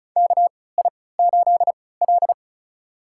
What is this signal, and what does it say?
Morse code, heard as a keyed tone, about 36 wpm, spaced as for 20 wpm, 700 Hz. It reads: KI8L
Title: USA Call Signs - 4 Characters - 3x Character Spacing - 35wpm